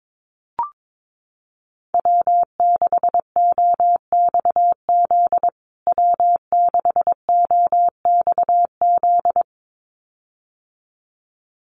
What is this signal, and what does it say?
Morse code, heard as a keyed tone, about 22 wpm, 700 Hz. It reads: W6OXZ W6OXZ
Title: USA Call Signs Repeated 2x 22wpm